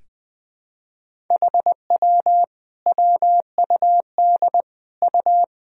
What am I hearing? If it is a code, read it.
Morse code, heard as a keyed tone, about 20 wpm, 700 Hz. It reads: HW WUD U
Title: Sentences from Top 100 Words and Abbreviations 20wpm